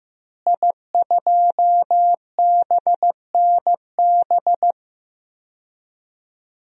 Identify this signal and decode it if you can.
Morse code, heard as a keyed tone, about 15 wpm, 700 Hz. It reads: I2BNB
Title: DXCC-100 Call Signs - 5 Characters - Repeated 2x - 15wpm